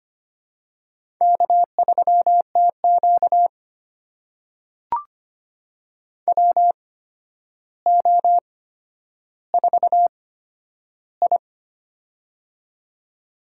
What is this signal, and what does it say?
Morse code, heard as a keyed tone, about 25 wpm, spaced as for 11 wpm, 700 Hz. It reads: K3TQ WO4I
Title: USA Call Signs - 4 Characters - 8x Character Spacing - 25wpm